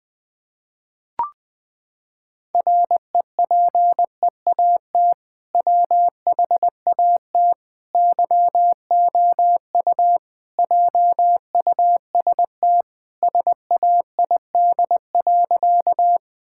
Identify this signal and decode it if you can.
Morse code, heard as a keyed tone, about 20 wpm, 700 Hz. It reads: REPEAT WHAT YOU JUST SAID.
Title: Sentences from Top 1600 Words 20wpm